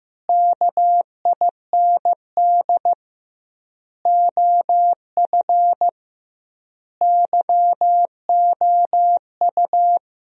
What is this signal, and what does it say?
Morse code, heard as a keyed tone, about 15 wpm, 700 Hz. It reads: KIND OF YOU
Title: 2x Word-Spacing - Sentences from Top 300 Words - 15wpm